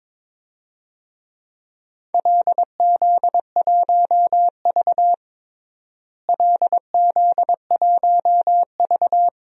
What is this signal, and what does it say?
Morse code, heard as a keyed tone, about 22 wpm, 700 Hz. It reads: LZ1V LZ1V
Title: DXCC-100 Call Signs - 4 Characters - Repeated 2x - 22wpm